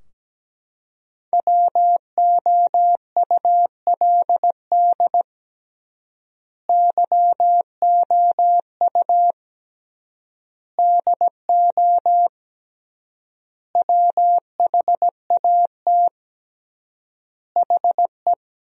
Morse code, heard as a keyed tone, about 17 wpm, 700 Hz. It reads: WOULD YOU DO WHAT HE
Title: 3x Word-Spacing - Sentences from Top 100 Words - 17wpm